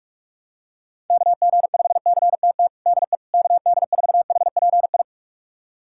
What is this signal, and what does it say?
Morse code, heard as a keyed tone, about 46 wpm, 700 Hz. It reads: KG5CTT DE KD4HPI
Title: USA Callsign DE Callsign 45wpm